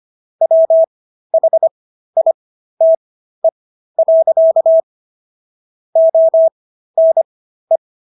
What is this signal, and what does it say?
Morse code, heard as a keyed tone, about 25 wpm, spaced as for 13 wpm, 650 Hz. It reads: WHITE. ONE